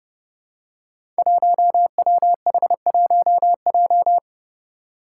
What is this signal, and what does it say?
Morse code, heard as a keyed tone, about 30 wpm, 700 Hz. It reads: 1WH1J
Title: Five Challenging Letter-Number Combinations 30wpm